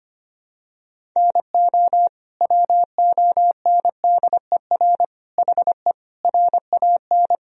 Morse code, heard as a keyed tone, about 25 wpm, 700 Hz. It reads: NO WONDER HE RAN
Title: Sentences from Top 1000 Words - Review Part 3 - 25wpm